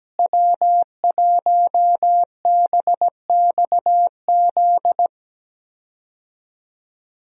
Morse code, heard as a keyed tone, about 17 wpm, 700 Hz. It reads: W1BXZ
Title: USA Call Signs - 5 Characters - Repeated 2x - 17wpm